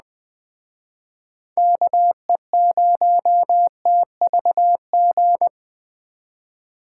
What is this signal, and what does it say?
Morse code, heard as a keyed tone, about 19 wpm, 700 Hz. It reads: KE0TVG